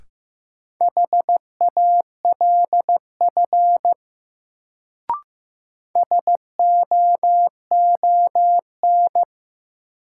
Morse code, heard as a keyed tone, about 15 wpm, 700 Hz. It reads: HALF SOON
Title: Sets of 3 Words - New Word Drill - 15wpm